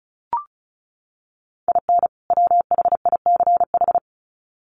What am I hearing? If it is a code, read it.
Morse code, heard as a keyed tone, about 34 wpm, 700 Hz. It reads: IN WHICH
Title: Sets of 2 Words 35wpm